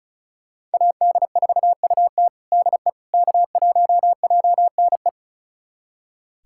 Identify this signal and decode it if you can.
Morse code, heard as a keyed tone, about 34 wpm, 700 Hz. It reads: AD4UT DE K1JNE
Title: USA Callsign DE Callsign - 5 Characters - 35wpm